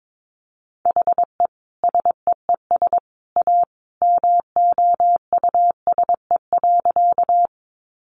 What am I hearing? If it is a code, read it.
Morse code, heard as a keyed tone, about 22 wpm, 700 Hz. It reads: HE SEES A MOUSE.
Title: Easy Phrases 22wpm